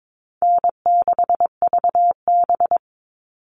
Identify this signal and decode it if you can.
Morse code, heard as a keyed tone, about 22 wpm, 700 Hz. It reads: N6VB
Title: All Cut Numbers - 4-Digit 22wpm